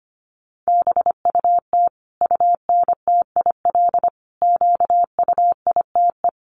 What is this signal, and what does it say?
Morse code, heard as a keyed tone, about 25 wpm, 700 Hz. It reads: BUT UNTIL QUITE
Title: Sentences from Top 1200 Words 25wpm